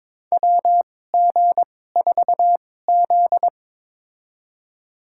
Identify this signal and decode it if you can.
Morse code, heard as a keyed tone, about 22 wpm, spaced as for 16 wpm, 700 Hz. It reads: WG4Z
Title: USA Call Signs - 4 Characters - 2x Character Spacing - 22wpm